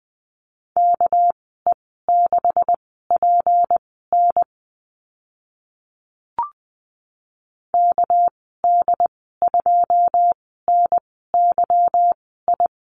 Morse code, heard as a keyed tone, about 20 wpm, spaced as for 14 wpm, 700 Hz. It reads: KE6PN KD2NYI
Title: 213 Lesson 34g - Number 8 - Call Signs 20wpm